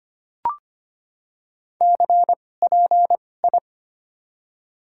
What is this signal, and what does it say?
Morse code, heard as a keyed tone, about 25 wpm, spaced as for 18 wpm, 700 Hz. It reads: CPI
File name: Top 90 QSO Elements - 2x Character Spacing - 25wpm